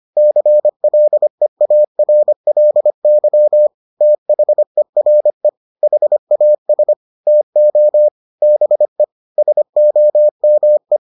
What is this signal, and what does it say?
Morse code, heard as a keyed tone, about 25 wpm, 600 Hz. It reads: CLEARLY THERE HAS TO BE SOME